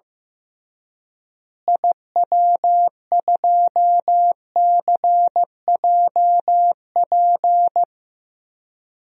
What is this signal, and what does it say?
Morse code, heard as a keyed tone, about 15 wpm, 700 Hz. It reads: IW2CJP
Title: DXCC-100 Call Signs - 6 Characters - Repeated 3x - 15wpm